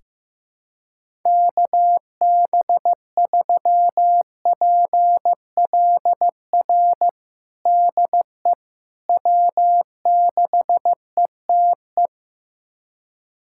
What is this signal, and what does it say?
Morse code, heard as a keyed tone, about 15 wpm, 700 Hz. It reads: KB3PLR DE W6ETE
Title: USA Callsign DE Callsign 15wpm